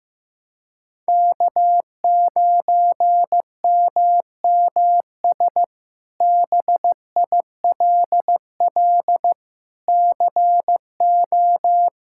Morse code, heard as a keyed tone, about 15 wpm, 700 Hz. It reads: K9MMS BILL CO
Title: K1USN SST QSO Practice - 15wpm